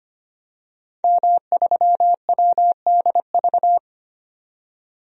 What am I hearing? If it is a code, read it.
Morse code, heard as a keyed tone, about 25 wpm, 700 Hz. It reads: M3WDV
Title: DXCC-100 Call Signs - 5 Characters - 25wpm